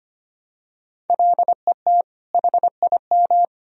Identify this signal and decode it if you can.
Morse code, heard as a keyed tone, about 25 wpm, 700 Hz. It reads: LET HIM
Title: Sentences from Top 1000 Words - Review Part 1 - 25wpm